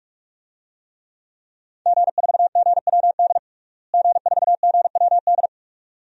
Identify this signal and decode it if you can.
Morse code, heard as a keyed tone, about 46 wpm, 700 Hz. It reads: G4GWD G4GWD
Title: DXCC-100 Call Signs - 5 Characters - Repeated 2x - 45wpm